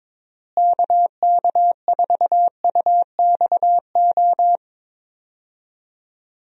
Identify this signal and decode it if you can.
Morse code, heard as a keyed tone, about 22 wpm, 700 Hz. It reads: KK4UXO
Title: USA Call Signs - 6 Characters - Repeated 3x - 22wpm